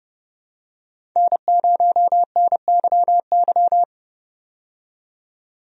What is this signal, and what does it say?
Morse code, heard as a keyed tone, about 30 wpm, 700 Hz. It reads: N0NYY
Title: USA Call Signs - 5 Characters - 30wpm